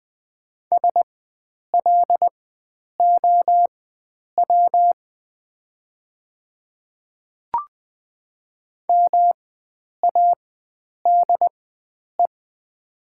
Morse code, heard as a keyed tone, about 20 wpm, spaced as for 9 wpm, 690 Hz. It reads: SLOW MADE
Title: 4x Character Spacing - Top 500 Words - 20wpm